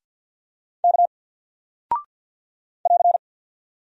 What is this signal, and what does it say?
Morse code, heard as a keyed tone, about 48 wpm, 700 Hz. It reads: K <AR>
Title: Single Character (Letter:Number:Punctuation:Prosign) Alt 50wpm